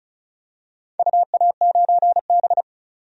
Morse code, heard as a keyed tone, about 36 wpm, 690 Hz. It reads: UA9B
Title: DXCC-100 Call Signs - 4 Characters - 35wpm